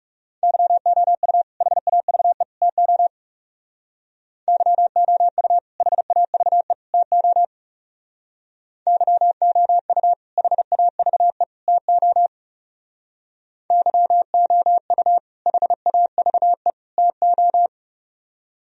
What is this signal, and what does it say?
Morse code, heard as a keyed tone, about 39 wpm, 700 Hz. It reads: YOU HAVE TO YOU HAVE TO YOU HAVE TO YOU HAVE TO